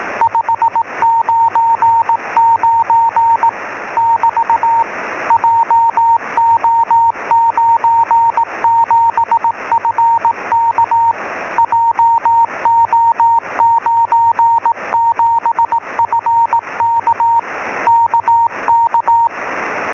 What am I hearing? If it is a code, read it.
Morse code, heard as a keyed tone, about 18 wpm, 920 Hz, with radio noise. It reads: 599 <BT> JO97FK JO97FK KK
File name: sm7lcb / 10-sm1a